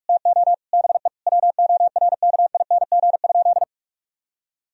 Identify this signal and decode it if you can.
Morse code, heard as a keyed tone, about 46 wpm, 700 Hz. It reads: TO BE WORKING?
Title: Sentences from Top 1000 Words - Review Part 4 - 45wpm